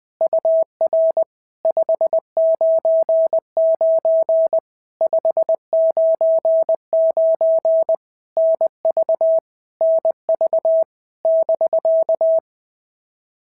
Morse code, heard as a keyed tone, about 20 wpm, 650 Hz. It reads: UR 599 599 NV NV <BK>
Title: POTA QSO Practice - 20wpm